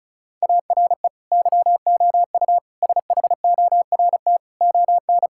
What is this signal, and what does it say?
Morse code, heard as a keyed tone, about 34 wpm, 700 Hz. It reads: ARE YOU SHORT ON
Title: Sentences from Top 1000 Words - Review Part 5 - 35wpm